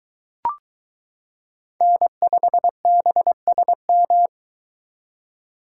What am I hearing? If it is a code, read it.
Morse code, heard as a keyed tone, about 22 wpm, 700 Hz. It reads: N5BSM